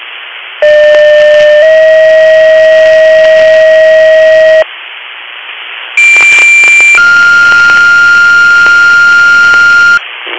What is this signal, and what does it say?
Two-tone page, heard as a one-second tone, then a three-second tone, about 600 then 640 Hz, about 2350 then 1400 Hz, with noise. It is heard from 0.6 to 10.0 s.